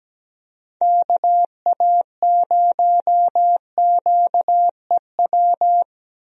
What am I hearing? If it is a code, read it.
Morse code, heard as a keyed tone, about 17 wpm, 700 Hz. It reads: KA0QEW
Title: USA Call Signs - 6 Characters - Repeated 2x - 17wpm